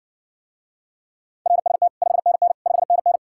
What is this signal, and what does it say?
Morse code, heard as a keyed tone, about 60 wpm, 700 Hz. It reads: RST 5NN 5NN